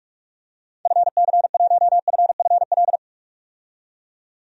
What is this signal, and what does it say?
Morse code, heard as a keyed tone, about 46 wpm, 700 Hz. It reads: FC1FFL
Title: DXCC-100 Call Signs - 6 Characters - 45wpm